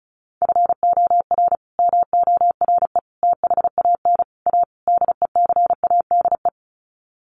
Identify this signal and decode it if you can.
Morse code, heard as a keyed tone, about 34 wpm, 700 Hz. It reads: FOR MORE THAN A DECADE
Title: Sets of 5 Words 35wpm